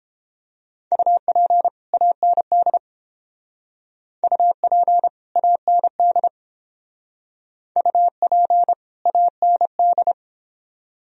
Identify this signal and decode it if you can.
Morse code, heard as a keyed tone, about 30 wpm, 700 Hz. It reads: UP AND UP AND UP AND